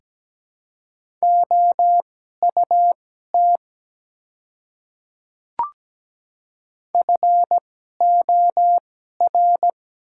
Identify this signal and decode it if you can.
Morse code, heard as a keyed tone, about 17 wpm, spaced as for 12 wpm, 690 Hz. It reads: OUT FOR
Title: 2x Character Spacing - Top 100 Words - 17wpm